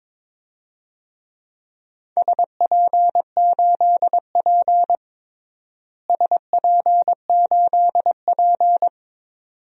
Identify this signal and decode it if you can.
Morse code, heard as a keyed tone, about 22 wpm, 700 Hz. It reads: SP8P SP8P
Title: DXCC-100 Call Signs - 4 Characters - Repeated 2x - 22wpm